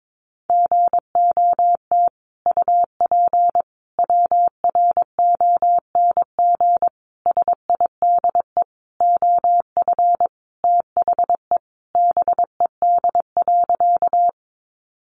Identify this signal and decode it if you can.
Morse code, heard as a keyed tone, about 22 wpm, 700 Hz. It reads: GOT UP WRONG SIDE OF THE BED.
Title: Sentences from Top 900 Words 22wpm